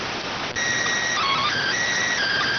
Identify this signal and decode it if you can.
SSTV (slow-scan television), PD 290.